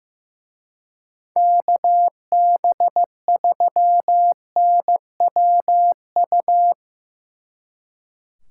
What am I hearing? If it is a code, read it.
Morse code, heard as a keyed tone, about 17 wpm, 700 Hz. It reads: KB3NWU
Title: Speed Racing - USA 2x3 Call Signs 15wpm